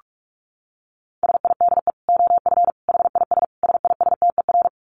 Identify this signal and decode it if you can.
Morse code, heard as a keyed tone, about 44 wpm, 700 Hz. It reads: SIDE OF HIS SISTER
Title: Dit Challenge 45wpm